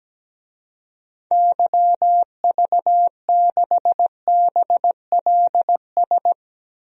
Morse code, heard as a keyed tone, about 17 wpm, 700 Hz. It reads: YV6BLS
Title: DXCC-100 Call Signs - 6 Characters - 17wpm